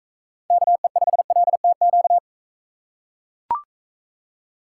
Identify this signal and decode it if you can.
Morse code, heard as a keyed tone, about 40 wpm, 700 Hz. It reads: KE5LTQ